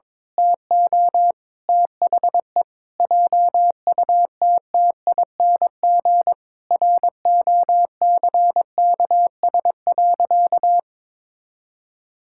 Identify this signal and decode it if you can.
Morse code, heard as a keyed tone, about 21 wpm, 700 Hz. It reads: TO THE JUTTING ROCKS.
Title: Pangram Challenge v2 - 22wpm